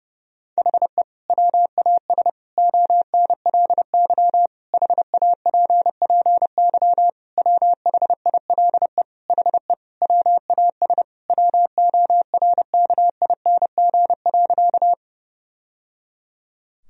Morse code, heard as a keyed tone, about 30 wpm, 700 Hz. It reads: HE WAS ONLY HAPPY WHILE HE WAS WORKING.
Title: Sentences from Top 1000 Words - Review Part 4 - 30wpm